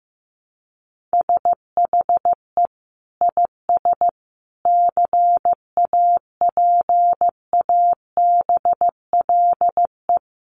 Morse code, heard as a keyed tone, about 15 wpm, 700 Hz. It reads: SHE IS CAPABLE
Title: Sentences from Top 1200 Words 15wpm